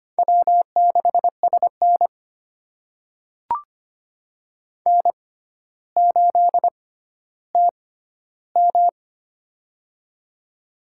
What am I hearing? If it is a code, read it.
Morse code, heard as a keyed tone, about 25 wpm, spaced as for 13 wpm, 700 Hz. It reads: W6SN N8TM
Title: USA Call Signs - 6x Character Spacing - 25wpm